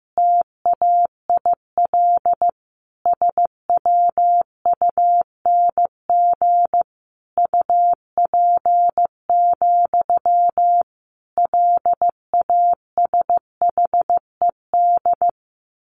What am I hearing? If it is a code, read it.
Morse code, heard as a keyed tone, about 15 wpm, 700 Hz. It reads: TAIL SWUNG UP, LASHED
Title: A Sound Of Thunder 15wpm